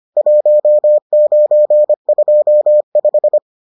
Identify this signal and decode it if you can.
Morse code, heard as a keyed tone, about 25 wpm, 600 Hz. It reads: 1925